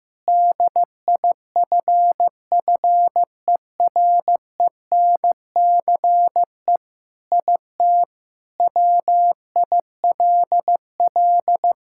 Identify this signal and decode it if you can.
Morse code, heard as a keyed tone, about 15 wpm, 700 Hz. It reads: DIFFERENCE IT WILL
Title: Sentences from Top 1000 Words - Review Part 5 - 15wpm